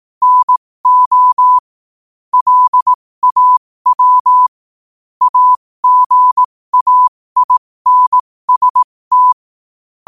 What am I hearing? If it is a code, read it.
Morse code, heard as a keyed tone, about 18 wpm, spaced as for 15 wpm, 1000 Hz. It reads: NO LAW AGAINST
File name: 15wpm